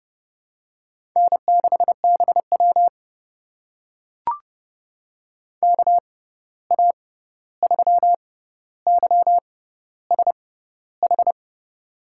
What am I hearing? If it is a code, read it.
Morse code, heard as a keyed tone, about 30 wpm, spaced as for 14 wpm, 700 Hz. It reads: N6BW KA3YSH
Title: USA Call Signs - 6x Character Spacing - 30wpm